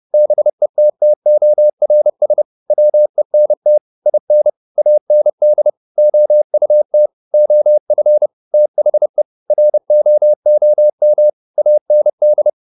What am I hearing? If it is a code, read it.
Morse code, heard as a keyed tone, about 30 wpm, 600 Hz. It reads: BETTORS WENT IN AND OUT OF THE ROOM AND